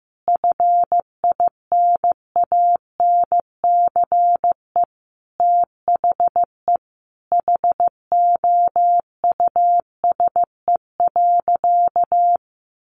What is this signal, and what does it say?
Morse code, heard as a keyed tone, about 15 wpm, 700 Hz. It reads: FINANCE THE HOUSE.
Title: Sentences from Top 1500 Words 15wpm